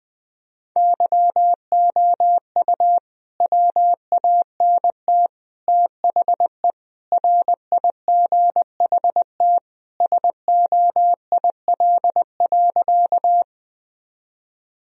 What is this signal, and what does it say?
Morse code, heard as a keyed tone, about 20 wpm, 700 Hz. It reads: YOU WANT THE RIGHT SOIL.
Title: Sentences from Top 1000 Words - Review Part 4 - 20wpm